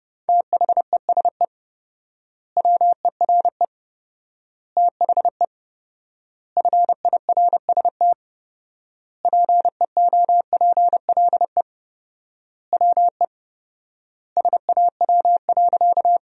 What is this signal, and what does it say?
Morse code, heard as a keyed tone, about 32 wpm, 700 Hz. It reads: THESE WERE THE FIRST PEOPLE WE SAW.
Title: 4x Word-Spacing - Sentences from Top 300 Words - 30wpm